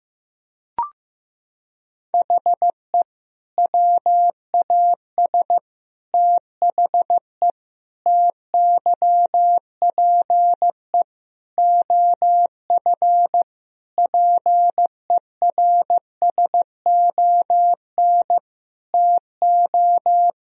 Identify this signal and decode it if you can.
Morse code, heard as a keyed tone, about 16 wpm, 700 Hz. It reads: HE WAS THE TYPE OF PERSON TO
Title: Sentences from Top 1000 Words - Review Part 3 - 15wpm